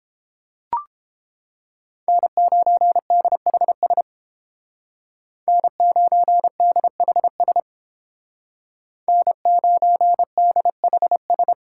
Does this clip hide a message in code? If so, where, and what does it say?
Yes: Morse code from 2.1 to 11.5 s, heard as a keyed tone, about 30 wpm, 700 Hz. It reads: N9DHS N9DHS N9DHS